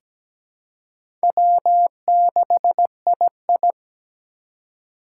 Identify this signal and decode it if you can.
Morse code, heard as a keyed tone, about 17 wpm, 700 Hz. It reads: W6II
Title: USA Call Signs - 6x Character Spacing - 17wpm